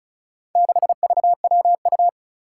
Morse code, heard as a keyed tone, about 36 wpm, 700 Hz. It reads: 6VWU
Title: Four Challenging Letter-Number Combinations 35wpm